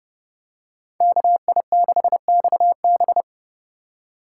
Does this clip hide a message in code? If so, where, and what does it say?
Yes: Morse code from 1.0 to 3.2 s, heard as a keyed tone, about 30 wpm, 700 Hz. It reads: KI6XB